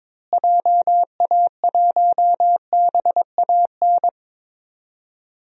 Morse code, heard as a keyed tone, about 22 wpm, 700 Hz. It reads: JA1BAN
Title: DXCC-100 Call Signs - 6 Characters - Repeated 2x - 22wpm